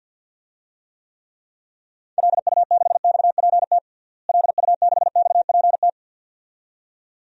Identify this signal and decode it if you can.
Morse code, heard as a keyed tone, about 51 wpm, 700 Hz. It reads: LU6XPT LU6XPT
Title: DXCC-100 Call Signs - 6 Characters - Repeated 2x - 50wpm